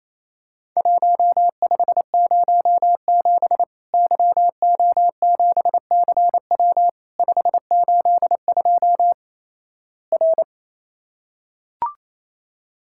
Morse code, heard as a keyed tone, about 28 wpm, 700 Hz. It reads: 1507 YO7CW 582 R
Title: Worked All Europe DX Contest - QTC Exchange - 28wpm